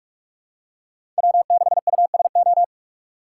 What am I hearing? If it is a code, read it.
Morse code, heard as a keyed tone, about 47 wpm, 700 Hz. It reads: W6USO
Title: USA Call Signs - 5 Characters - 4x Character Spacing - 45wpm